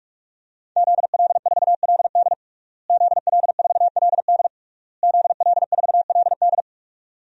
Morse code, heard as a keyed tone, about 46 wpm, 700 Hz. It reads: ZL4LD ZL4LD ZL4LD
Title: DXCC-100 Call Signs - 5 Characters - Repeated 3x - 45wpm